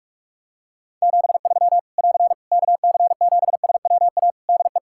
Morse code, heard as a keyed tone, about 44 wpm, 700 Hz. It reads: 73 <AR> KC7SWA DE
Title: Practice QSO - Exchange 4 - Ending - 45wpm